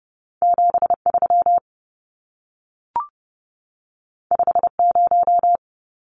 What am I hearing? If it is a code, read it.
Morse code, heard as a keyed tone, about 30 wpm, 700 Hz. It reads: 73 50
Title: Numbers - 2-Digit 30wpm